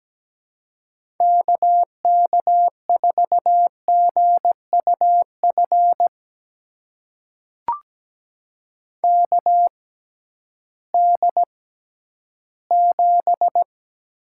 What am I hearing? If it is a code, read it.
Morse code, heard as a keyed tone, about 17 wpm, spaced as for 11 wpm, 700 Hz. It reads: KK4GUF KD7
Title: USA Call Signs - 6 Characters - 6x Character Spacing - 17wpm